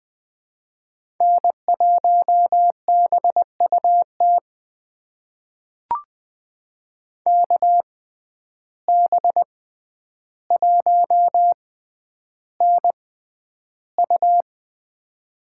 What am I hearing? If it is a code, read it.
Morse code, heard as a keyed tone, about 20 wpm, spaced as for 10 wpm, 700 Hz. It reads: N1BUT KB1NU
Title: USA Call Signs - 5 Characters - 6x Character Spacing - 20wpm